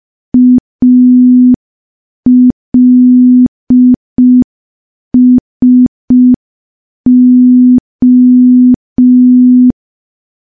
Morse code, heard as a keyed tone, about 5 wpm, 250 Hz. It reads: ALSO
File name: morse / morse-training-05WPM